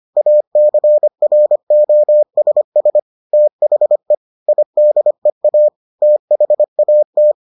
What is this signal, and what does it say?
Morse code, heard as a keyed tone, about 25 wpm, 600 Hz. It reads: ACROSS THE IDEA THAT